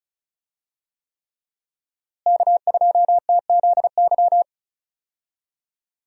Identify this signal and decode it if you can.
Morse code, heard as a keyed tone, about 34 wpm, 700 Hz. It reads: K2TZY
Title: USA Call Signs - 5 Characters - Repeated 3x - 35wpm